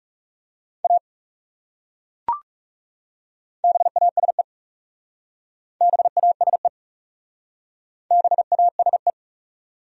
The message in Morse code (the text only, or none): A BASE BASE BASE